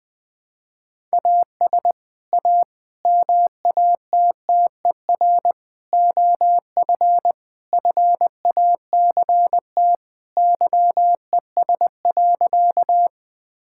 Morse code, heard as a keyed tone, about 20 wpm, 700 Hz. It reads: AS A MATTER OF FACT YES.
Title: Sentences with 4-Word Phrases - 20wpm